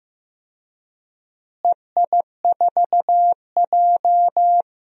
Morse code, heard as a keyed tone, about 15 wpm, 700 Hz. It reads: EI4J
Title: DXCC-100 Call Signs - 4 Characters - Repeated 3x - 15wpm